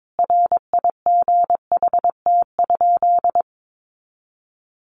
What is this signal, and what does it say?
Morse code, heard as a keyed tone, about 22 wpm, 700 Hz. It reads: RIGHT?
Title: Sentences from Top 200 Words 22wpm